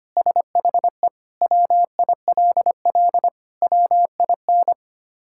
Morse code, heard as a keyed tone, about 25 wpm, 700 Hz. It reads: SHE WILL WIN
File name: practice / 25wpm / Sentences with 3-Word Phrases - 25wpm